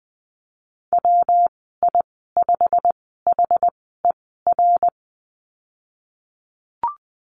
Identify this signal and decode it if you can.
Morse code, heard as a keyed tone, about 19 wpm, spaced as for 14 wpm, 700 Hz. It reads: WI5HER
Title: 193 Lesson 31g - Letter v - Call Signs 20wpm